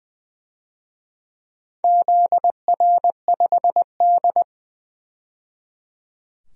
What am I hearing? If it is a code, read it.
Morse code, heard as a keyed tone, about 20 wpm, 700 Hz. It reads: ZR5D
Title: DXCC-100 Call Signs - 4 Characters - 20wpm